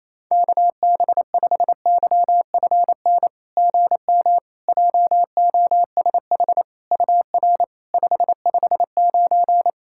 Morse code, heard as a keyed tone, about 28 wpm, 700 Hz. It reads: KB5YFN GM JOSH UR 559
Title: SOTA QSO Practice 28wpm